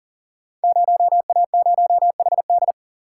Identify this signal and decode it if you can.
Morse code, heard as a keyed tone, about 40 wpm, 700 Hz. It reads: 0A0HD
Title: Code Groups - 5 Letter:Numbers - 40wpm